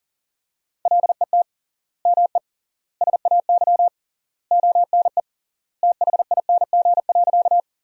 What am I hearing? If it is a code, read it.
Morse code, heard as a keyed tone, about 40 wpm, 700 Hz. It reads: LET ME SAY ONE THING.
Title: 3x Word-Spacing - Sentences from Top 300 Words - 40wpm